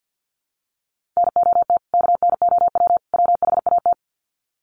Morse code, heard as a keyed tone, about 50 wpm, 700 Hz. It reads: NOT KNOW WHAT
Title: Sets of 3 Words 50wpm